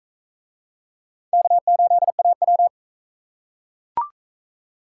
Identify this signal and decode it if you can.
Morse code, heard as a keyed tone, about 40 wpm, 700 Hz. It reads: K8AW